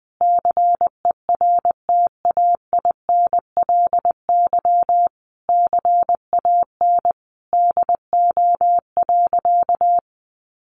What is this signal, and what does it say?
Morse code, heard as a keyed tone, about 19 wpm, 700 Hz. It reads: CERTAINLY CAN DO.
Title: Sentences from Top 900 Words 20wpm